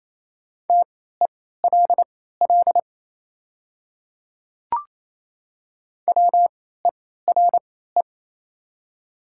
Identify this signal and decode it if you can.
Morse code, heard as a keyed tone, about 27 wpm, spaced as for 16 wpm, 700 Hz. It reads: TELL WERE
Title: 3x Character Spacing - Top 300 Words - 28wpm